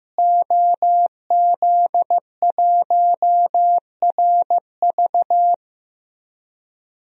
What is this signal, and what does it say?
Morse code, heard as a keyed tone, about 15 wpm, 700 Hz. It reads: OZ1RV
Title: DXCC-100 Call Signs - 5 Characters - Repeated 2x - 15wpm